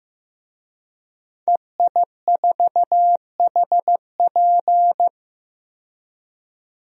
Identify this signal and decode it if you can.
Morse code, heard as a keyed tone, about 16 wpm, 700 Hz. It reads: EI4HP
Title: DXCC-100 Call Signs - 5 Characters - 15wpm